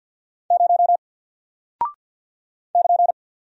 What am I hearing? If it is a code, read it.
Morse code, heard as a keyed tone, about 49 wpm, 700 Hz. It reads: 0 <KN>
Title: Single Character (Letter:Number:Punctuation:Prosign) Alt 50wpm